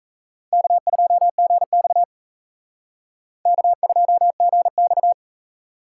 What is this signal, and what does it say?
Morse code, heard as a keyed tone, about 35 wpm, 700 Hz. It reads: K2GX K2GX
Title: Speed Racing - USA 1x2 Call Signs 28wpm